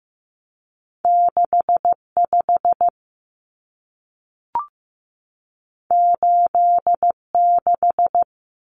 Morse code, heard as a keyed tone, about 15 wpm, 700 Hz. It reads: 65 86
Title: Common Cut Numbers - 2-Digit 15wpm